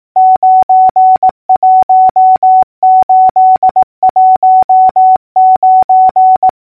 Morse code, heard as a keyed tone, about 18 wpm, 750 Hz. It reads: 91819
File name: numbers / V2-morse-clicks